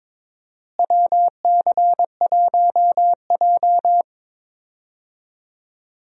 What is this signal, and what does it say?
Morse code, heard as a keyed tone, about 22 wpm, 700 Hz. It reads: WC1J
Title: USA Call Signs - 4 Characters - Repeated 3x - 22wpm